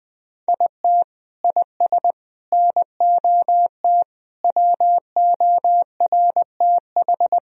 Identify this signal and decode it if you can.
Morse code, heard as a keyed tone, about 20 wpm, 700 Hz. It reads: IT IS NOT WORTH
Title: Sentences from Top 1000 Words - Review Part 3 - 20wpm